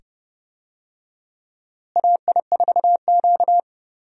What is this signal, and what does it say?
Morse code, heard as a keyed tone, about 30 wpm, 700 Hz. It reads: AI4Q
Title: USA Call Signs - 4 Characters - Repeated 2x - 30wpm